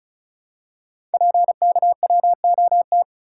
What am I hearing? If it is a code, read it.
Morse code, heard as a keyed tone, about 36 wpm, 700 Hz. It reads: PKWOT
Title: Code Groups - 5 Letters - 35wpm